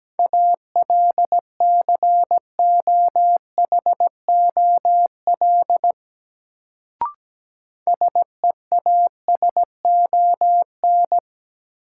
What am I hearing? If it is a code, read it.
Morse code, heard as a keyed tone, about 17 wpm, 700 Hz. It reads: ALCOHOL SEASON
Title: Sentences from Top 1400 Words - New Word Drill - 17wpm